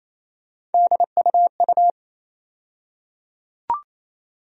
Morse code, heard as a keyed tone, about 28 wpm, 700 Hz. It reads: DUU
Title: USA Call Sign Postfixes - 3 Letters 28wpm